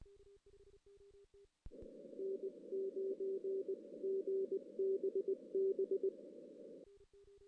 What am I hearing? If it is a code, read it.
Morse code, heard as a keyed tone, about 20 wpm, spaced as for 17 wpm, 390 Hz. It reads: N9GBB
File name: 2016 JAN CW NAQP / 2016 Jan naqp k5ot - 0-6